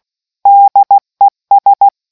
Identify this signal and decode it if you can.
Morse code, heard as a keyed tone, about 16 wpm, 800 Hz. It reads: DES